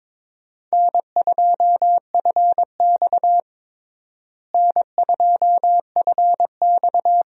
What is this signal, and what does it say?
Morse code, heard as a keyed tone, about 22 wpm, 700 Hz. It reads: N2FX N2FX
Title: USA Call Signs - 4 Characters - Repeated 2x - 22wpm